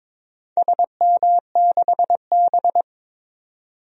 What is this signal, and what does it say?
Morse code, heard as a keyed tone, about 22 wpm, 700 Hz. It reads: SM6B
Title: DXCC-100 Call Signs - 4 Characters - 22wpm